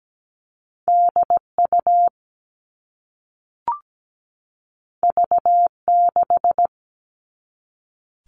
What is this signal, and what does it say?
Morse code, heard as a keyed tone, about 17 wpm, 700 Hz. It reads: DU V6
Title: All Cut Numbers - 2-Digit 17wpm